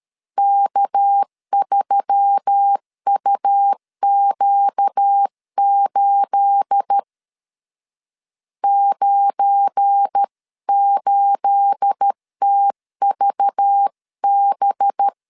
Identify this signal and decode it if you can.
Morse code, heard as a keyed tone, about 13 wpm, 800 Hz. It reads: K3UQ8 98TVB
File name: LEZIONI / 9 28 12wpm 2351